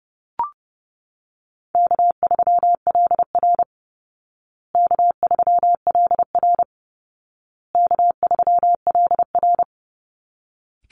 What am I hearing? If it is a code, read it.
Morse code, heard as a keyed tone, about 30 wpm, 700 Hz. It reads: K3LR K3LR K3LR
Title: Easy Call Signs Repeated 3x - 30wpm